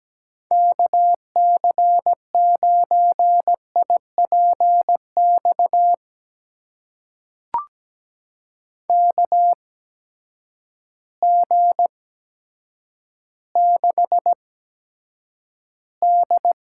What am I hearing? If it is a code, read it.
Morse code, heard as a keyed tone, about 17 wpm, spaced as for 9 wpm, 700 Hz. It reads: KC9IPX KG6D
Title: USA Call Signs - 6 Characters - 8x Character Spacing - 17wpm